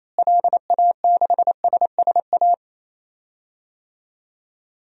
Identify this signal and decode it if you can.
Morse code, heard as a keyed tone, about 28 wpm, 700 Hz. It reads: LA6SSA
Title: DXCC-100 Call Signs - 6 Characters - Repeated 2x - 28wpm